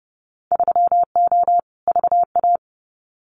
Morse code, heard as a keyed tone, about 30 wpm, 700 Hz. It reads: 3O VA
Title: Winter Field Day Exchanges 30wpm